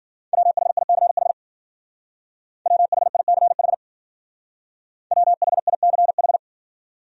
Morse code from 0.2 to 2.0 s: WHICH

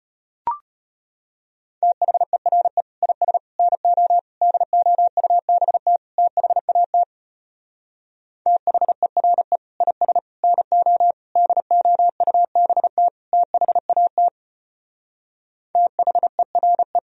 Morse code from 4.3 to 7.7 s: DOUBT THAT

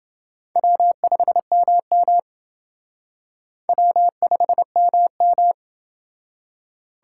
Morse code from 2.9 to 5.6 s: W5MM

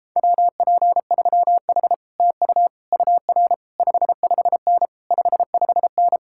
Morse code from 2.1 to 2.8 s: TU

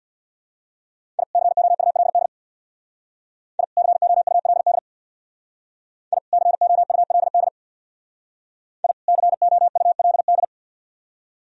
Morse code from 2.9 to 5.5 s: I COULD

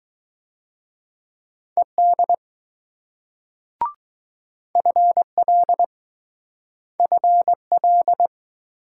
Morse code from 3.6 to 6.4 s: FL

